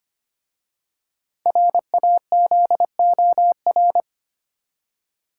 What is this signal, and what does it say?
Morse code, heard as a keyed tone, about 25 wpm, 700 Hz. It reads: RAZOR
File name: practice / 25wpm / Easy Words with Hints - 25wpm